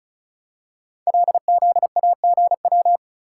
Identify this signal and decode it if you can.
Morse code, heard as a keyed tone, about 36 wpm, 700 Hz. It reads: LZAGW
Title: Code Groups - 5 Letter:Numbers - 35wpm